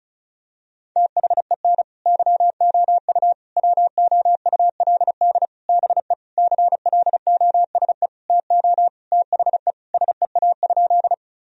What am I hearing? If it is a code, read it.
Morse code, heard as a keyed tone, about 36 wpm, 700 Hz. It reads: THEN YOU WOULD BE CLOSE TO THE SEA?
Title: Sentences from Top 1000 Words - Review Part 1 - 35wpm